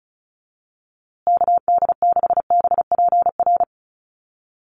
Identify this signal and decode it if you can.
Morse code, heard as a keyed tone, about 34 wpm, 700 Hz. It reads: KD6BPR
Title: USA Call Signs Repeated 2x 35wpm